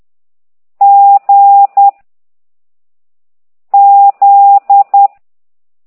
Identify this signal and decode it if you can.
Morse code, heard as a keyed tone, about 10 wpm, 800 Hz. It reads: G Z
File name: ct2idw / exerc15